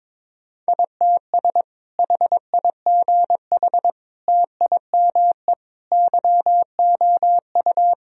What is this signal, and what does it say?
Morse code, heard as a keyed tone, about 22 wpm, 700 Hz. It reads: ITS HIGH TIME YOU
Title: Sentences from Top 1000 Words - Review Part 1 - 22wpm